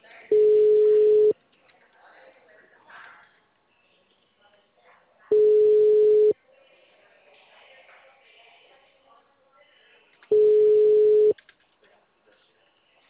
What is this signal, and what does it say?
Ringback tone (Europe, 425 Hz), with speech.